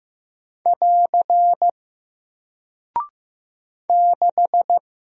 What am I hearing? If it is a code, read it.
Morse code, heard as a keyed tone, about 15 wpm, 700 Hz. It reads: <AR> 6
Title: Single Character (Letter:Number:Punctuation:Prosign) 15wpm